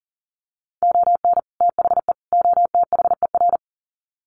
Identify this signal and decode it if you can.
Morse code, heard as a keyed tone, about 40 wpm, 700 Hz. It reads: ON THE OTHER
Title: Sets of 3 Words 40wpm